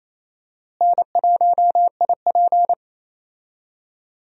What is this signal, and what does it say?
Morse code, heard as a keyed tone, about 28 wpm, 700 Hz. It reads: N1IP
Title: USA Call Signs - 4 Characters - Repeated 3x - 28wpm